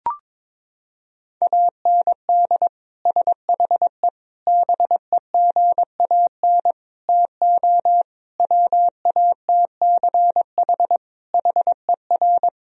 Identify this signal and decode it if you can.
Morse code, heard as a keyed tone, about 22 wpm, 700 Hz. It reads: AND SHE BEGAN TO WATCH HER
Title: Sentences from Top 1000 Words - Review Part 1 - 22wpm